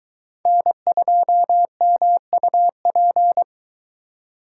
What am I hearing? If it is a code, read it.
Morse code, heard as a keyed tone, about 22 wpm, 700 Hz. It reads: N2MUP